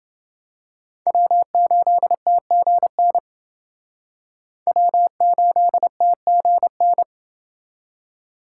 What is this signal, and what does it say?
Morse code, heard as a keyed tone, about 29 wpm, 700 Hz. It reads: W8TGN W8TGN